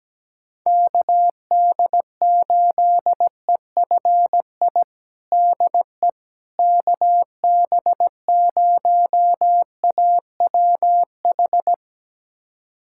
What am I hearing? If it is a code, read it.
Morse code, heard as a keyed tone, about 17 wpm, 700 Hz. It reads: KD8EFI DE KB0AWH
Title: USA Callsign DE Callsign - 6 Characters - 17wpm